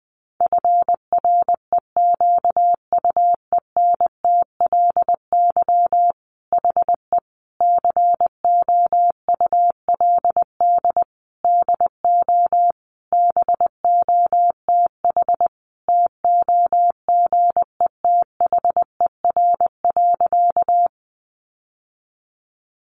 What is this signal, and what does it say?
Morse code, heard as a keyed tone, about 19 wpm, 700 Hz. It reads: FREQUENTLY HE COULD DO BOTH TOGETHER.
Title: Sentences from Top 1300 Words 20wpm